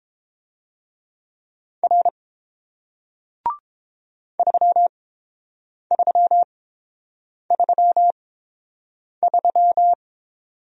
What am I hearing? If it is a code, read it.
Morse code, heard as a keyed tone, about 30 wpm, spaced as for 7 wpm, 700 Hz. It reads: R 3333